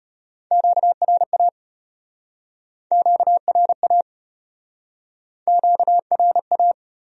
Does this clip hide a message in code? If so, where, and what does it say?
Yes: Morse code from 0.5 to 6.7 s, heard as a keyed tone, about 34 wpm, 700 Hz. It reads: QRA QRA QRA